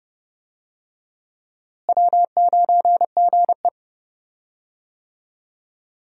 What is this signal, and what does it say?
Morse code, heard as a keyed tone, about 30 wpm, 700 Hz. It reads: W9GE
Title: USA Call Signs - 4 Characters - Repeated 2x - 30wpm